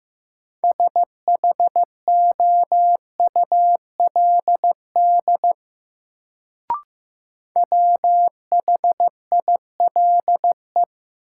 Morse code, heard as a keyed tone, about 15 wpm, 700 Hz. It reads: SHOULD WHILE
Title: Sentences from Top 300 Words - New Word Drill - 15wpm